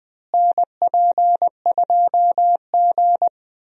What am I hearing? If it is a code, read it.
Morse code, heard as a keyed tone, about 20 wpm, 700 Hz. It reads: NP2G
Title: USA Call Signs - 4 Characters - Repeated 3x - 20wpm